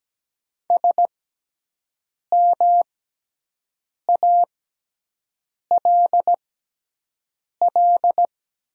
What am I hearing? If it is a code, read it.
Morse code, heard as a keyed tone, about 17 wpm, spaced as for 6 wpm, 700 Hz. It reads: SMALL
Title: 6x Character Spacing - Top 200 Words - 17wpm